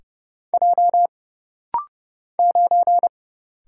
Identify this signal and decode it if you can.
Morse code, heard as a keyed tone, about 30 wpm, 700 Hz. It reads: J 9
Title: Single Letter-Number - Rapid-Fire 30wpm